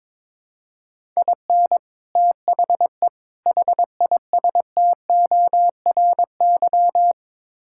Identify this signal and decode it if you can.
Morse code, heard as a keyed tone, about 22 wpm, 700 Hz. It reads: IN THE HISTORY
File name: practice / 22wpm / Build A Phrase - 5-Word Phrases - 22wpm